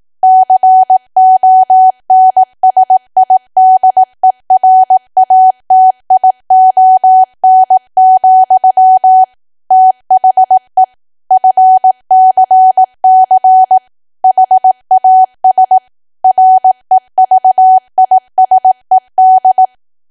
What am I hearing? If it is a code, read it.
Morse code, heard as a keyed tone, about 18 wpm, 750 Hz. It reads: CONSIDERATION, THE FCC HAS REVISED
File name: cwa / W1AW-Level2 / 18WPM